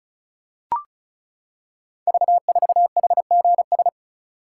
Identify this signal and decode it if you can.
Morse code, heard as a keyed tone, about 36 wpm, 700 Hz. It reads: V4HGS
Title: Code Groups - 5 Letter:Numbers - 35wpm